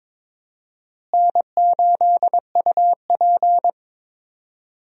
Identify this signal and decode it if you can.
Morse code, heard as a keyed tone, about 22 wpm, 700 Hz. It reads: N8UP
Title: USA Call Signs - 4 Characters - Repeated 3x - 22wpm